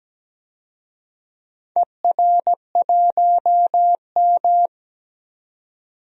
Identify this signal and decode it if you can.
Morse code, heard as a keyed tone, about 17 wpm, 700 Hz. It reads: ER1M
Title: DXCC-100 Call Signs - 4 Characters - 17wpm